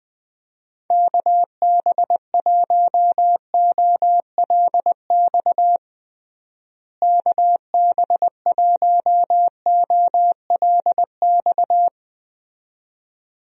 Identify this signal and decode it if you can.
Morse code, heard as a keyed tone, about 20 wpm, 700 Hz. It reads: KB1OLX KB1OLX
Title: USA Call Signs - 6 Characters - Repeated 2x - 20wpm